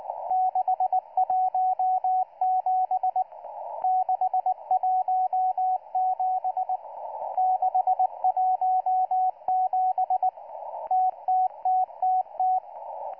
Morse code, heard as a keyed tone, about 19 wpm, 740 Hz, with radio noise. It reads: 617 617 617 TTTTT